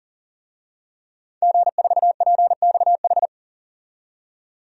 Morse code, heard as a keyed tone, about 40 wpm, 700 Hz. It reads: G4PXH
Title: DXCC-100 Call Signs - 5 Characters - 40wpm